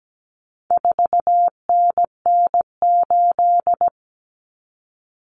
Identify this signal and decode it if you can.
Morse code, heard as a keyed tone, about 17 wpm, 700 Hz. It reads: 4NN8
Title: Common Cut Numbers - 4-Digit 17wpm